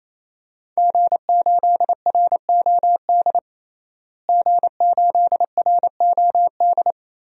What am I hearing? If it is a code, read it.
Morse code, heard as a keyed tone, about 27 wpm, 700 Hz. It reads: G8ROD G8ROD
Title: DXCC-100 Call Signs - 5 Characters - Repeated 3x - 28wpm